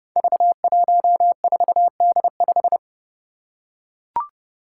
Morse code, heard as a keyed tone, about 29 wpm, 700 Hz. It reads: V14D5
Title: Five Challenging Letter-Number Combinations 30wpm